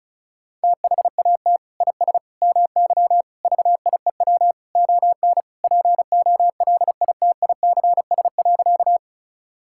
Morse code, heard as a keyed tone, about 36 wpm, 700 Hz. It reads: THAT IS MY VIEW ON POLITICS.
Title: Sentences from Top 1000 Words - Review Part 6 - 35wpm